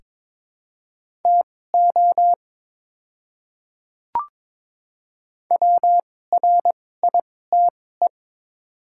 Morse code, heard as a keyed tone, about 22 wpm, spaced as for 16 wpm, 700 Hz. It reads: TO WRITE
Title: 2x Character Spacing - Top 200 Words - 22wpm